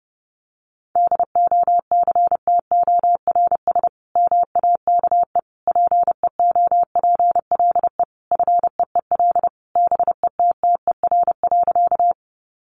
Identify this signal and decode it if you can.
Morse code, heard as a keyed tone, about 30 wpm, 700 Hz. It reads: DOCTORS MAKE PEOPLE FEEL BETTER.
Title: Easy Phrases 30wpm